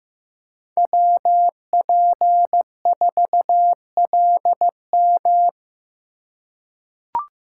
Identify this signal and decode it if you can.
Morse code, heard as a keyed tone, about 15 wpm, 700 Hz. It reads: WP4LM
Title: USA Call Signs - 3x Character Spacing - 15wpm